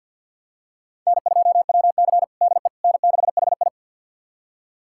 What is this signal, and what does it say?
Morse code, heard as a keyed tone, about 52 wpm, 700 Hz. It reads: N2WC DE N6HI
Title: USA Callsign DE Callsign - 4 Characters - 50wpm